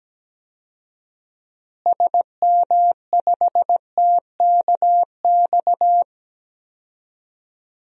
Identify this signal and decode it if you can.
Morse code, heard as a keyed tone, about 17 wpm, 700 Hz. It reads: SM5TKX
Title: DXCC-100 Call Signs - 6 Characters - Repeated 2x - 17wpm